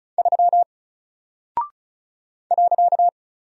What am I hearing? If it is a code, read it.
Morse code, heard as a keyed tone, about 36 wpm, 700 Hz. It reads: 3 .
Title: Single Character (Letter:Number:Punctuation:Prosign) Alt 35wpm